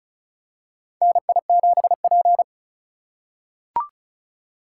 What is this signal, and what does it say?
Morse code, heard as a keyed tone, about 34 wpm, 700 Hz. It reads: NI7P
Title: USA Call Signs - 4 Characters - 3x Character Spacing - 35wpm